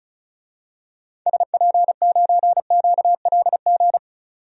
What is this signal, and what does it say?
Morse code, heard as a keyed tone, about 36 wpm, 700 Hz. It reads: SP9QLG
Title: DXCC-100 Call Signs - 6 Characters - 35wpm